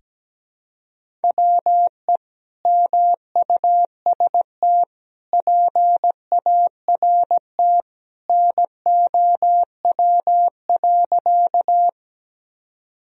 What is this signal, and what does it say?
Morse code, heard as a keyed tone, about 17 wpm, 700 Hz. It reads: WE MUST PART NOW.
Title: Sentences from Top 1000 Words - Review Part 1 - 17wpm